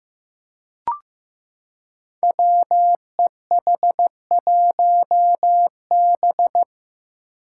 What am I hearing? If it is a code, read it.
Morse code, heard as a keyed tone, about 15 wpm, 700 Hz. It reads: WEH1B
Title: Five Challenging Letter-Number Combinations 15wpm